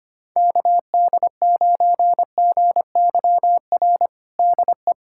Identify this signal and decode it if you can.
Morse code, heard as a keyed tone, about 25 wpm, 700 Hz. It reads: KD9GYR DE
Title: USA Callsign DE Callsign - 6 Characters - 25wpm